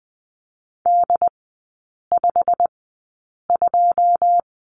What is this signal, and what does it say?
Morse code, heard as a keyed tone, about 20 wpm, spaced as for 8 wpm, 700 Hz. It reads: D52
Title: 205 Lesson 33e - Letter j - Three Characters 20wpm